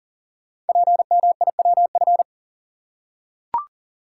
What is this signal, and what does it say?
Morse code, heard as a keyed tone, about 40 wpm, 700 Hz. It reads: PMIWF
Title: Code Groups - 5 Letter:Numbers - 40wpm